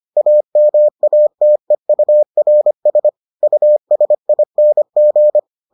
Morse code, heard as a keyed tone, about 25 wpm, 600 Hz. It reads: AMATEURS USING